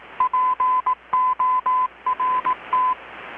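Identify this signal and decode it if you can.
Morse code, heard as a keyed tone, about 18 wpm, 1010 Hz. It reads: PORT
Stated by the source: Morse from KPH's Night of Nights XX maritime radiotelegraphy event, received in Chiapas, Mexico about 3,800 km away on a Kenwood R-600 with a grounded 10 m long-wire antenna.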